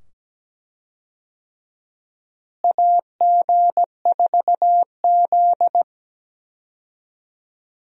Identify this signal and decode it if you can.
Morse code, heard as a keyed tone, about 17 wpm, 700 Hz. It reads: AG4Z
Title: USA Call Signs - 4 Characters - Repeated 2x - 17wpm